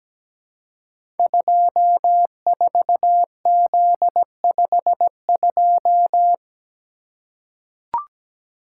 Morse code, heard as a keyed tone, about 17 wpm, 700 Hz. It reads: 24Z52
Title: Code Groups - 5 Letter:Numbers - 17wpm